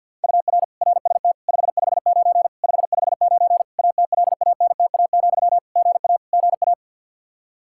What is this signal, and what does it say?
Morse code, heard as a keyed tone, about 50 wpm, 700 Hz. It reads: UR RST 559 559 ATLANTA, GA GA
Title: K1USN SST QSO Practice 50wpm